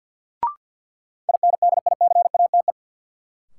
Morse code, heard as a keyed tone, about 53 wpm, 700 Hz. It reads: INDICATE
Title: Sentences from Top 1300 Words - New Word Drill Alt - 50wpm